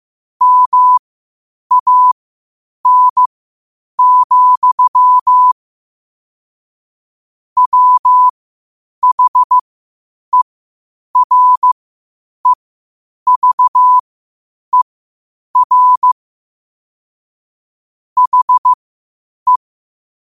Morse code, heard as a keyed tone, about 15 wpm, spaced as for 8 wpm, 1000 Hz. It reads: MAN, WHEREVER HE
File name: qotdfiles / 8wpm